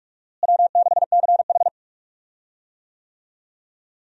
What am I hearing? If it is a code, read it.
Morse code, heard as a keyed tone, about 46 wpm, 700 Hz. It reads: W6CH
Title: USA Call Signs - 4 Characters - Repeated 3x - 45wpm